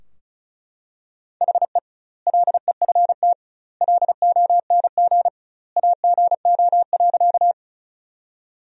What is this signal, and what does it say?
Morse code, heard as a keyed tone, about 34 wpm, 700 Hz. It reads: HE LEFT LONG AGO.
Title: 2x Word-Spacing - Sentences from Top 500 Words - 35wpm